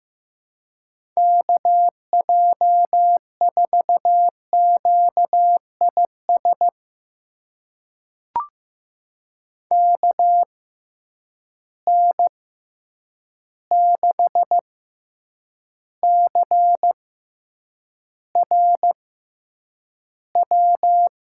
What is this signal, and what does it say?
Morse code, heard as a keyed tone, about 15 wpm, spaced as for 8 wpm, 700 Hz. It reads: KJ4QIS KN6CRW
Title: USA Call Signs - 6 Characters - 6x Character Spacing - 15wpm